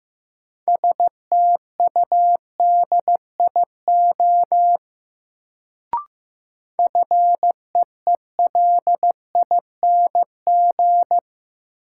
Morse code, heard as a keyed tone, about 15 wpm, 700 Hz. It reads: STUDIO FEELING
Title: Sentences from Top 1200 Words - New Word Drill Alt - 15wpm